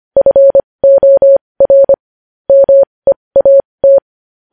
Morse code, heard as a keyed tone, about 25 wpm, spaced as for 20 wpm, 550 Hz. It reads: FOR MEAT
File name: podcast / News headlines-1x-25x20